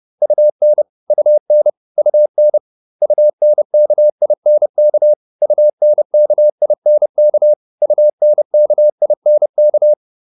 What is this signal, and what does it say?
Morse code, heard as a keyed tone, about 30 wpm, 600 Hz. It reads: UN UN UN UNKINK UNKINK UNKINK